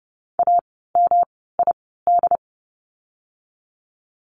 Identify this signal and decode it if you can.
Morse code, heard as a keyed tone, about 30 wpm, spaced as for 17 wpm, 700 Hz. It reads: AMID
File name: course / 30:10wpm / 108 Lesson 18f - Letter m - Words 30wpm